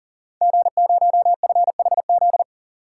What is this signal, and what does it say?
Morse code, heard as a keyed tone, about 38 wpm, 700 Hz. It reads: G0FHZ
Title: Code Groups - 5 Letter:Numbers - 40wpm